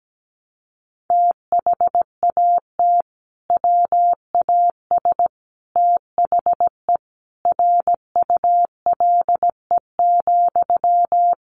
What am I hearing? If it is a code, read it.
Morse code, heard as a keyed tone, about 17 wpm, 700 Hz. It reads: THAT WAS THE RULE,
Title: Sentences from Top 400 Words 17wpm